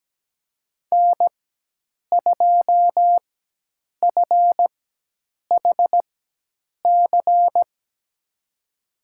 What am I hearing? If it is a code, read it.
Morse code, heard as a keyed tone, about 17 wpm, spaced as for 8 wpm, 700 Hz. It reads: N2FHC
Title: USA Call Signs - 5 Characters - 4x Character Spacing - 17wpm